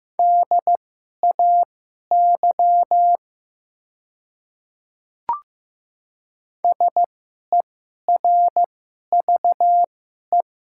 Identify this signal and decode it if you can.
Morse code, heard as a keyed tone, about 15 wpm, spaced as for 11 wpm, 690 Hz. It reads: DAY SERVE
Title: 2x Character Spacing - Top 500 Words - 15wpm